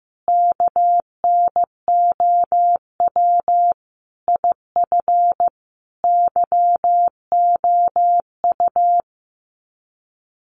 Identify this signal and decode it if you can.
Morse code, heard as a keyed tone, about 15 wpm, 700 Hz. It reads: KNOW IF YOU
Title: Sets of 5 Words 15wpm